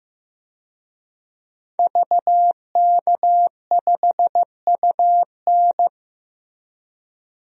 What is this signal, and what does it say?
Morse code, heard as a keyed tone, about 15 wpm, 700 Hz. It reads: VK5UN
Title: DXCC-100 Call Signs - 5 Characters - Repeated 3x - 15wpm